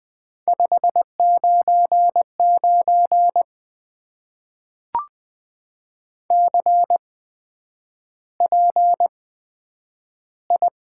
Morse code, heard as a keyed tone, about 20 wpm, spaced as for 9 wpm, 700 Hz. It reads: 599 CPI
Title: Top 90 QSO Elements - 8x Character Spacing - 20wpm